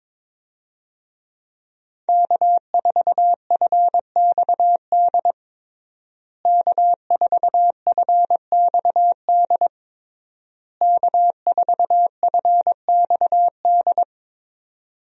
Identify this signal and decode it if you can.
Morse code, heard as a keyed tone, about 22 wpm, 700 Hz. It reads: K4FXD K4FXD K4FXD
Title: USA Call Signs - 5 Characters - Repeated 3x - 22wpm